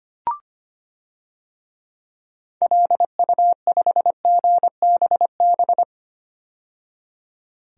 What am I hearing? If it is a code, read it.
Morse code, heard as a keyed tone, about 25 wpm, 700 Hz. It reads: LU5GBB
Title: DXCC-100 Call Signs - 6 Characters - 25wpm